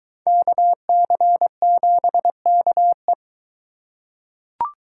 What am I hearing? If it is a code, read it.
Morse code, heard as a keyed tone, about 22 wpm, 700 Hz. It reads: KC7KE